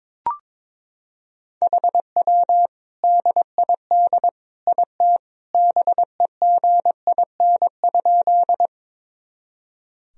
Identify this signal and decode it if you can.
Morse code, heard as a keyed tone, about 22 wpm, 700 Hz. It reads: HW DID IT BEGIN?
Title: Sentences from Top 300 Words and Abbreviations 22wpm